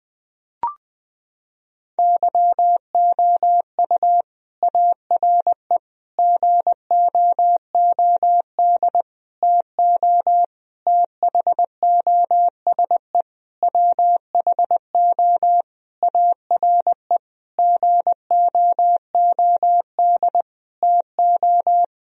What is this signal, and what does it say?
Morse code, heard as a keyed tone, about 20 wpm, 700 Hz. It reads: YOU ARE GOOD TO THOSE WHO ARE GOOD TO
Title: Sentences from Top 1000 Words - Review Part 1 - 20wpm